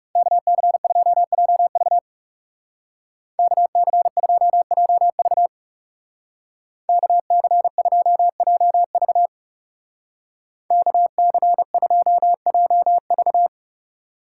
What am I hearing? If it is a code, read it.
Morse code, heard as a keyed tone, about 36 wpm, 700 Hz. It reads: KC2JV KC2JV KC2JV KC2JV
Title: Speed Racing - USA 2x2 Call Signs 30wpm